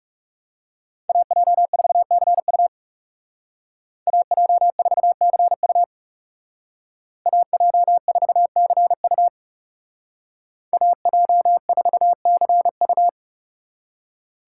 Morse code, heard as a keyed tone, about 39 wpm, 700 Hz. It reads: AJ4CU AJ4CU AJ4CU AJ4CU